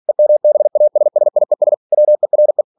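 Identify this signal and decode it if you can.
Morse code, heard as a keyed tone, about 46 wpm, 600 Hz. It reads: EMBASSIES WERE
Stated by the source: Practice text from an English novel sent at 47 WPM.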